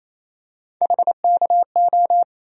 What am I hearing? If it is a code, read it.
Morse code, heard as a keyed tone, about 28 wpm, 700 Hz. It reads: HKO
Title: USA Call Sign Postfixes - 3 Letters 28wpm